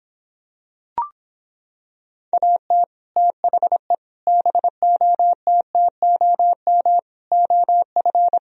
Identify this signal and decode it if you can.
Morse code, heard as a keyed tone, about 26 wpm, 700 Hz. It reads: AT THE BOTTOM OF